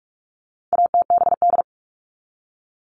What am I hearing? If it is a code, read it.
Morse code, heard as a keyed tone, about 44 wpm, 700 Hz. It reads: ATBD
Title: All Cut Numbers - 4-Digit 45wpm